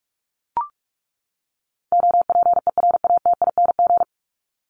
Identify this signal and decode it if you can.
Morse code, heard as a keyed tone, about 44 wpm, 700 Hz. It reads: OPERATING
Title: ING Suffix - Top 100 Words with Base Words 45wpm